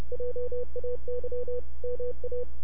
Morse code, heard as a keyed tone, about 30 wpm, 490 Hz. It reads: JAY MA
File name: NAQPCW-8-2009 / W1UJ NAQP CW 09 2300Z